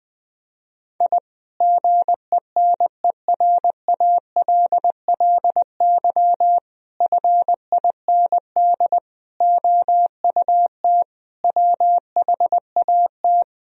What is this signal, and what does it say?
Morse code, heard as a keyed tone, about 20 wpm, 700 Hz. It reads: I GENERALLY FIND OUT WHAT
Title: Sentences from Top 1000 Words - Review Part 4 - 20wpm